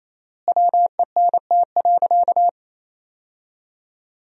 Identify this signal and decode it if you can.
Morse code, heard as a keyed tone, about 28 wpm, 700 Hz. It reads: WENT.
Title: 4x Word-Spacing - Sentences from Top 200 Words - 28wpm